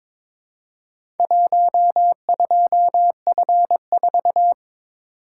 Morse code, heard as a keyed tone, about 22 wpm, 700 Hz. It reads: 12F4
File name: practice / 22wpm / Four Letter-Number Combinations - 22wpm